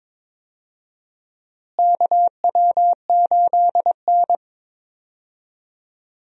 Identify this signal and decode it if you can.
Morse code, heard as a keyed tone, about 22 wpm, 700 Hz. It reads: KW8N
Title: USA Call Signs - 4 Characters - Repeated 2x - 22wpm